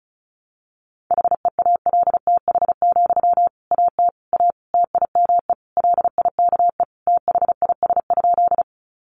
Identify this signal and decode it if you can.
Morse code, heard as a keyed tone, about 34 wpm, 700 Hz. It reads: HEALTH, AT A TIME LIKE THIS?
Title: Sentences from Top 600 Words 35wpm